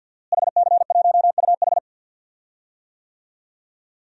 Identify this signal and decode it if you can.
Morse code, heard as a keyed tone, about 52 wpm, 700 Hz. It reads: HC1UH
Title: DXCC-100 Call Signs - 5 Characters - Repeated 3x - 50wpm